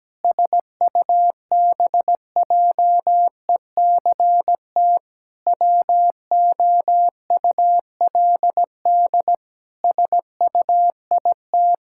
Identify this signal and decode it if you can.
Morse code, heard as a keyed tone, about 17 wpm, 700 Hz. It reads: SUBJECT WOULD SUIT
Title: Sentences from Top 1600 Words 17wpm